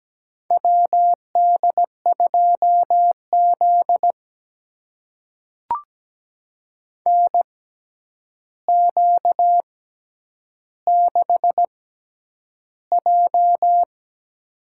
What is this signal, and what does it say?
Morse code, heard as a keyed tone, about 17 wpm, spaced as for 9 wpm, 700 Hz. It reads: WD2Z NQ6J
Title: USA Call Signs - 4 Characters - 6x Character Spacing - 17wpm